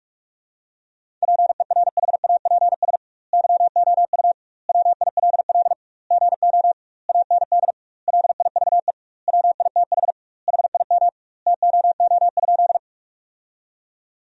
Morse code, heard as a keyed tone, about 43 wpm, 700 Hz. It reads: PERHAPS YOU WILL GO AND LIVE WITH HIM TOO?
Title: 2x Word-Spacing - Sentences from Top 500 Words - 45wpm